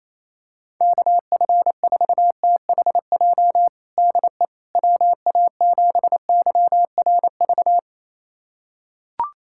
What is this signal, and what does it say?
Morse code, heard as a keyed tone, about 28 wpm, 700 Hz. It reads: KF4THJ DE WA7YRV
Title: USA Callsign DE Callsign 28wpm